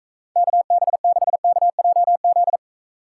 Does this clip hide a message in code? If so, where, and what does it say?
Yes: Morse code from 0.4 to 2.6 s, heard as a keyed tone, about 44 wpm, 700 Hz. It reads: KB6KJZ